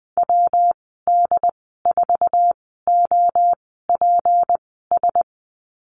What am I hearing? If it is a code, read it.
Morse code, heard as a keyed tone, about 20 wpm, spaced as for 14 wpm, 700 Hz. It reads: WD4OPS
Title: 193 Lesson 31g - Letter v - Call Signs 20wpm